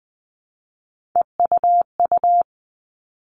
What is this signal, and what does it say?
Morse code, heard as a keyed tone, about 20 wpm, 700 Hz. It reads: EUU
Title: All Cut Numbers - 3-Digit 20wpm